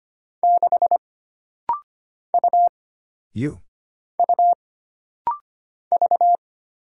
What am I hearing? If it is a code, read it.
Morse code, heard as a keyed tone, about 25 wpm, 700 Hz. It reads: 6 U U V
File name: Single Character (Letter:Number:Punctuation:Prosign) - Rapid-Fire - 25wpm